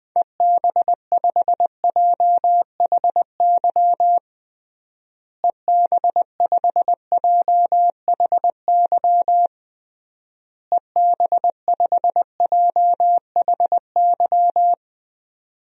Morse code, heard as a keyed tone, about 19 wpm, 700 Hz. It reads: EB5JHY EB5JHY EB5JHY
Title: DXCC-100 Call Signs - 6 Characters - Repeated 3x - 20wpm